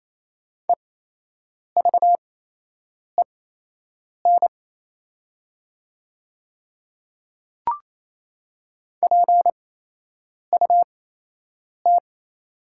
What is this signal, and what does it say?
Morse code, heard as a keyed tone, about 28 wpm, spaced as for 8 wpm, 700 Hz. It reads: EVEN PUT
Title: 8x Character Spacing - Top 300 Words - 28wpm